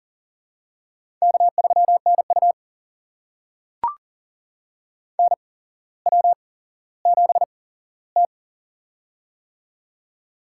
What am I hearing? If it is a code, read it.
Morse code, heard as a keyed tone, about 40 wpm, spaced as for 17 wpm, 700 Hz. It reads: K3NU NW7T
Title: USA Call Signs - 8x Character Spacing - 40wpm